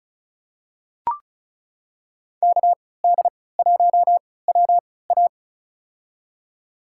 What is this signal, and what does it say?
Morse code, heard as a keyed tone, about 34 wpm, spaced as for 20 wpm, 700 Hz. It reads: KD1WA
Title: USA Call Signs - 3x Character Spacing - 35wpm